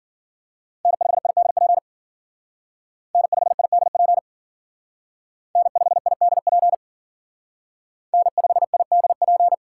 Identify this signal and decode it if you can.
Morse code, heard as a keyed tone, about 50 wpm, 700 Hz. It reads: N5IDP N5IDP N5IDP N5IDP
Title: Speed Racing - USA 1x3 Call Signs 40wpm